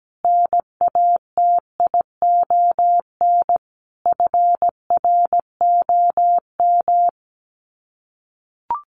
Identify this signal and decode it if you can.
Morse code, heard as a keyed tone, about 17 wpm, 700 Hz. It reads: NATION FROM
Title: Sets of 5 Words 17wpm